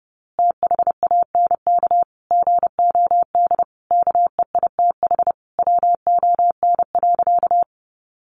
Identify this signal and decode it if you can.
Morse code, heard as a keyed tone, about 30 wpm, 700 Hz. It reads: THANK GOD KEITH WON.
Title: A Sound Of Thunder 30wpm